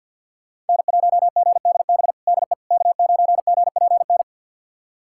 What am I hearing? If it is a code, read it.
Morse code, heard as a keyed tone, about 52 wpm, 690 Hz. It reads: N1GDB DE K9GPN
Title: USA Callsign DE Callsign - 5 Characters - 50wpm